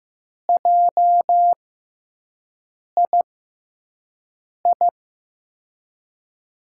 Morse code, heard as a keyed tone, about 15 wpm, spaced as for 5 wpm, 700 Hz. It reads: JII
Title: USA Call Signs - 6 Characters - 6x Character Spacing - 15wpm